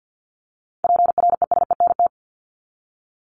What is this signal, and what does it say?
Morse code, heard as a keyed tone, about 50 wpm, 700 Hz. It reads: PRESENT
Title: ED Suffix - Top 100 Words with Base Words 50wpm